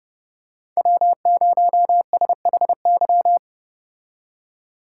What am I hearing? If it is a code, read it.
Morse code, heard as a keyed tone, about 29 wpm, 700 Hz. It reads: W0SHY